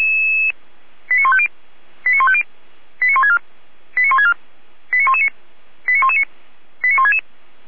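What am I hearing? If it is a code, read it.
Five-tone selective call (ZVEI), heard as quick runs of five tones, tones of about 70 ms. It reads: 89150 89150 89164 89164 89119 89119 89166
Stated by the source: ZVEI five-tone call received by SDR, from the Feuerwehr Orlamünde (fire brigade) server.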